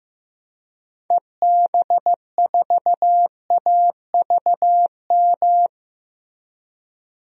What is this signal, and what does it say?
Morse code, heard as a keyed tone, about 15 wpm, 700 Hz. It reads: EB4AVM
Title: DXCC-100 Call Signs - 6 Characters - Repeated 2x - 15wpm